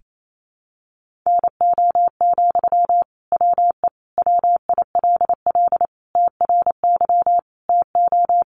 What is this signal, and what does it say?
Morse code, heard as a keyed tone, about 28 wpm, 700 Hz. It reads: NO, WE WILL TRY TO
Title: Sentences from Top 1100 Words 28wpm